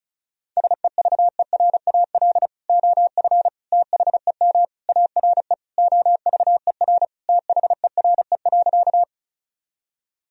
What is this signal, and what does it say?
Morse code, heard as a keyed tone, about 34 wpm, 700 Hz. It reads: SEVERAL OF THEM ARE OVER THERE.
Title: Sentences from Top 1000 Words - Review Part 2 - 35wpm